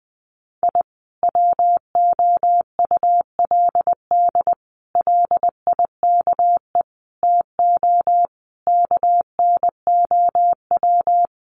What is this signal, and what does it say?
Morse code, heard as a keyed tone, about 20 wpm, 700 Hz. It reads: I WOULD LIKE TO KNOW
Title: Sets of 5 Words Encore 20wpm